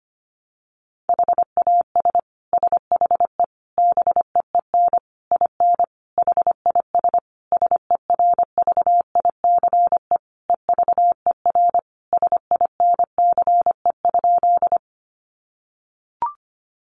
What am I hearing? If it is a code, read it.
Morse code, heard as a keyed tone, about 25 wpm, 700 Hz. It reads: HAS SHE BEEN IN HIS SERVICE EVER SINCE?
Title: Sentences from Top 700 Words 25wpm